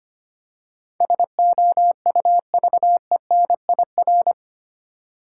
Morse code, heard as a keyed tone, about 26 wpm, 690 Hz. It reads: SOUVENIR
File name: practice / 25wpm / Easy Words with Hints - 25wpm